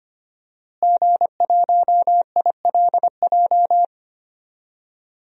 Morse code, heard as a keyed tone, about 25 wpm, 700 Hz. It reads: G1ILJ
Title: DXCC-100 Call Signs - 5 Characters - Repeated 2x - 25wpm